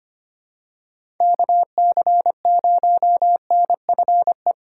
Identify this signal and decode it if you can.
Morse code, heard as a keyed tone, about 25 wpm, 700 Hz. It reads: KC0NFE
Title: USA Call Signs - 3x Character Spacing - 25wpm